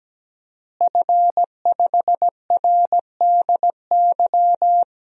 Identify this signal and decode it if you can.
Morse code, heard as a keyed tone, about 17 wpm, 700 Hz. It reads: F5RDY
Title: DXCC-100 Call Signs - 5 Characters - Repeated 3x - 17wpm